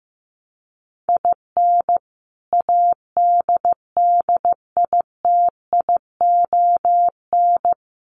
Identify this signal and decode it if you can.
Morse code, heard as a keyed tone, about 15 wpm, 700 Hz. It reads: IN ADDITION
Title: Sets of 4 Words 15wpm